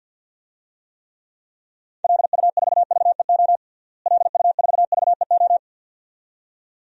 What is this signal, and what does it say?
Morse code, heard as a keyed tone, about 52 wpm, 700 Hz. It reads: LU4VEO LU4VEO
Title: DXCC-100 Call Signs - 6 Characters - Repeated 2x - 50wpm